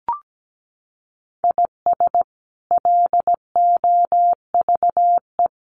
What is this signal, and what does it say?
Morse code, heard as a keyed tone, about 17 wpm, 700 Hz. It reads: IS LOVE
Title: Sentences from Top 1200 Words 17wpm